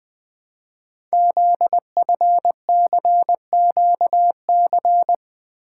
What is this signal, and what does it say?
Morse code, heard as a keyed tone, about 20 wpm, 700 Hz. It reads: ZFCQC
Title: Code Groups - 5 Letters - 20wpm